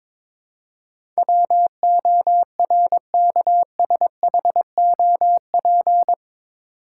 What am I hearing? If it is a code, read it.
Morse code, heard as a keyed tone, about 22 wpm, 700 Hz. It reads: WORKSHOP
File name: practice / 22wpm / Eight Letter Words - 22wpm